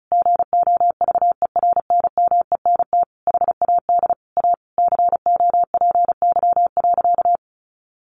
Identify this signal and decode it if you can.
Morse code, heard as a keyed tone, about 34 wpm, 700 Hz. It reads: GOVERNMENT HAD A COPY.
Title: Sentences from Top 1400 Words 35wpm